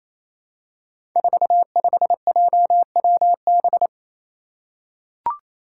Morse code, heard as a keyed tone, about 28 wpm, 700 Hz. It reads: 45JWB
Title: Five Challenging Letter-Number Combinations 28wpm